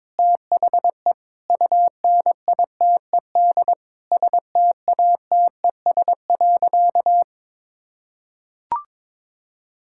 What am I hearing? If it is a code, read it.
Morse code, heard as a keyed tone, about 22 wpm, 700 Hz. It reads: THE UNITED STATES.
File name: Sentences with 9-Word Phrases - 22wpm